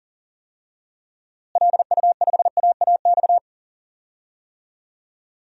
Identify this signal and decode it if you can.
Morse code, heard as a keyed tone, about 37 wpm, 700 Hz. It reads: LU5AAX
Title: DXCC-100 Call Signs - 6 Characters - Repeated 3x - 40wpm